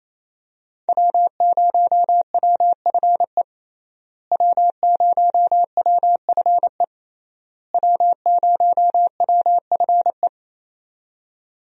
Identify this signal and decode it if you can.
Morse code, heard as a keyed tone, about 28 wpm, 700 Hz. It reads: W0WFE W0WFE W0WFE
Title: USA Call Signs - 5 Characters - Repeated 3x - 28wpm